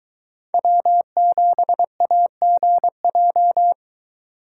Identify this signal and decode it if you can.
Morse code, heard as a keyed tone, about 21 wpm, 700 Hz. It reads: W7AGJ